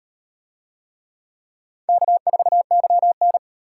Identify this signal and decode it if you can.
Morse code, heard as a keyed tone, about 36 wpm, 700 Hz. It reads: K4YN